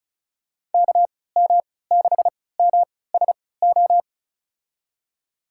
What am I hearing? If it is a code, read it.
Morse code, heard as a keyed tone, about 34 wpm, spaced as for 20 wpm, 700 Hz. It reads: KM6MSO
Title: USA Call Signs - 6 Characters - 3x Character Spacing - 35wpm